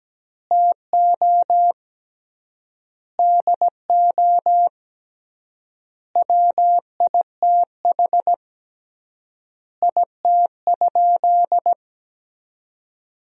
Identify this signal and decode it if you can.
Morse code, heard as a keyed tone, about 17 wpm, 700 Hz. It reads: TO DO WITH IT?
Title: 3x Word-Spacing - Sentences from Top 100 Words - 17wpm